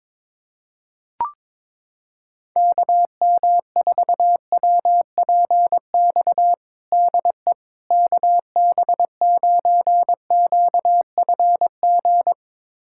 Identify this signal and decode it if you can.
Morse code, heard as a keyed tone, about 22 wpm, 700 Hz. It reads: KM4WPX DE KB9QFG
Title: USA Callsign DE Callsign - 6 Characters - 22wpm